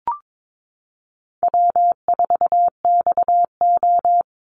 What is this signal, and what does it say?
Morse code, heard as a keyed tone, about 22 wpm, 700 Hz. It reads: W4XO
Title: Easy Call Signs Repeated 3x - 22wpm